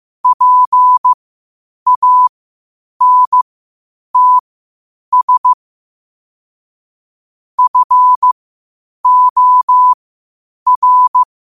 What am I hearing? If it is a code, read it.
Morse code, heard as a keyed tone, about 15 wpm, spaced as for 8 wpm, 1000 Hz. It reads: PANTS FOR
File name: qotdfiles / 8wpm